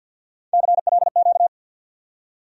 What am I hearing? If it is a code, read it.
Morse code, heard as a keyed tone, about 49 wpm, 700 Hz. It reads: CLQ
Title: USA Call Sign Postfixes - 3 Letters 50wpm